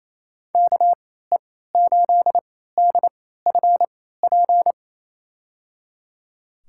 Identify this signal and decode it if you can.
Morse code, heard as a keyed tone, about 28 wpm, spaced as for 16 wpm, 700 Hz. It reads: KE8DFP
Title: USA Call Signs - 6 Characters - 3x Character Spacing - 28wpm